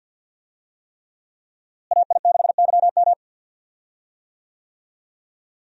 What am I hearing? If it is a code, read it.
Morse code, heard as a keyed tone, about 51 wpm, 700 Hz. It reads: AI6YM
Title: USA Call Signs - 5 Characters - Repeated 3x - 50wpm